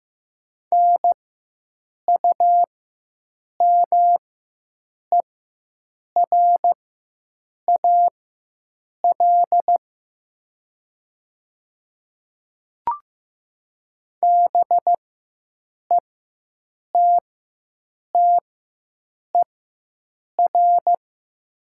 Morse code, heard as a keyed tone, about 15 wpm, spaced as for 7 wpm, 700 Hz. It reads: NUMERAL BETTER
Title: 4x Character Spacing - Top 500 Words - 15wpm